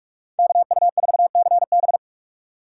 Morse code, heard as a keyed tone, about 46 wpm, 700 Hz. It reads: KU4CB
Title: USA Call Signs - 5 Characters - 3x Character Spacing - 45wpm